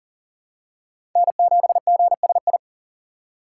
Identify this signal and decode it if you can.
Morse code, heard as a keyed tone, about 40 wpm, 700 Hz. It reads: N7GSI
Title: USA Call Signs - 5 Characters - 6x Character Spacing - 40wpm